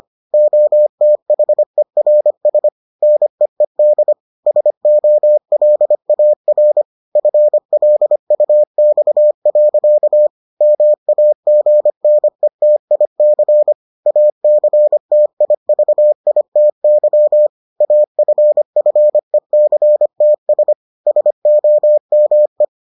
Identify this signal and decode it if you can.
Morse code, heard as a keyed tone, about 25 wpm, 600 Hz. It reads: OTHERS NEED SOLAR FLUX. MAGNETIC ACTIVITY AFFECTS SOME